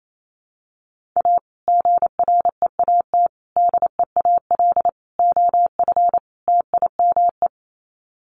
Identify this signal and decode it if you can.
Morse code, heard as a keyed tone, about 28 wpm, 700 Hz. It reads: A GREAT DEAL OF TIME
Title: Sets of 5 Words Encore2 28wpm